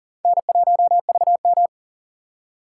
Morse code, heard as a keyed tone, about 40 wpm, 700 Hz. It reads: N1VM
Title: USA Call Signs - 4 Characters - 4x Character Spacing - 40wpm